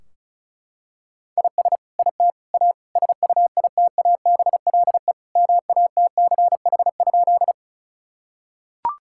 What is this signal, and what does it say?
Morse code, heard as a keyed tone, about 34 wpm, 700 Hz. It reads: IS IT A SUITABLE MATCH?
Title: Sentences from Top 1600 Words 35wpm